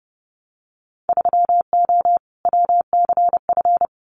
Morse code, heard as a keyed tone, about 30 wpm, 700 Hz. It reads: 3O WCF
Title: Winter Field Day Exchanges 30wpm